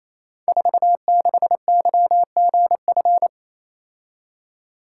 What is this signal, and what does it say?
Morse code, heard as a keyed tone, about 28 wpm, 700 Hz. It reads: 46YGF
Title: Code Groups - 5 Letter:Numbers - 28wpm